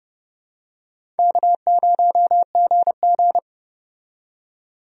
Morse code, heard as a keyed tone, about 30 wpm, 700 Hz. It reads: K0GG
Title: USA Call Signs - 4 Characters - 30wpm